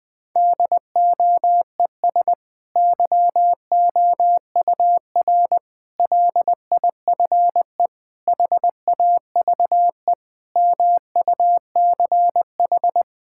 Morse code, heard as a keyed tone, about 21 wpm, 700 Hz. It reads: DOES YOUR LIFE HAVE MUCH
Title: Sentences from Top 1000 Words - Review Part 6 - 20wpm